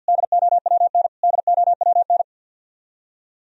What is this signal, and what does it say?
Morse code, heard as a keyed tone, about 47 wpm, 700 Hz. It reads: DOWN DOWN
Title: Top 100 Words Repeated 50wpm